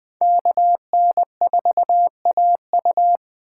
Morse code, heard as a keyed tone, about 24 wpm, 700 Hz. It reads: KN4AU
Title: Speed Racing - USA 2x2 Call Signs 20wpm